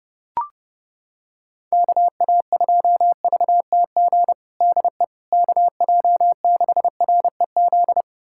Morse code, heard as a keyed tone, about 30 wpm, 700 Hz. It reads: KA2VTG DE KJ6REZ
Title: USA Callsign DE Callsign - 6 Characters - 30wpm